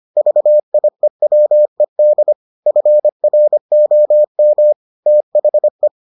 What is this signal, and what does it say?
Morse code, heard as a keyed tone, about 25 wpm, 600 Hz. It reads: VIEWED FROM THE